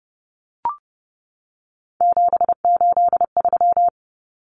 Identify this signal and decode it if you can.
Morse code, heard as a keyed tone, about 30 wpm, 700 Hz. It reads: 783
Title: Common Cut Numbers - 3-Digit 30wpm